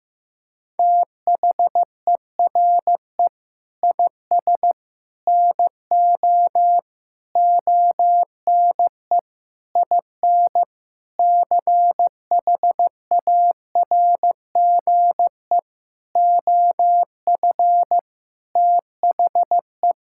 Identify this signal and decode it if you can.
Morse code, heard as a keyed tone, about 16 wpm, 700 Hz. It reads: THERE IS NO ONE IN CHARGE OF THE
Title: Sentences from Top 1000 Words - Review Part 6 - 15wpm